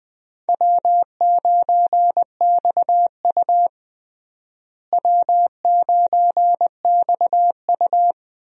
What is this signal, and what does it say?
Morse code, heard as a keyed tone, about 20 wpm, 700 Hz. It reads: W9XU W9XU
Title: USA Call Signs - 4 Characters - Repeated 2x - 20wpm